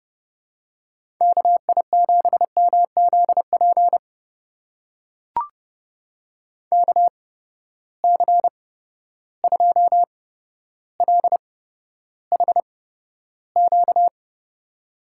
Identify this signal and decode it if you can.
Morse code, heard as a keyed tone, about 32 wpm, spaced as for 13 wpm, 700 Hz. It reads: KI7MZP KC2LHQ
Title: USA Call Signs - 6 Characters - 8x Character Spacing - 30wpm